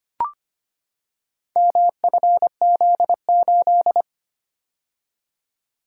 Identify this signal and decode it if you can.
Morse code, heard as a keyed tone, about 25 wpm, 700 Hz. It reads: MFZ8
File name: practice / 25wpm / Four Letter-Number Combinations - 25wpm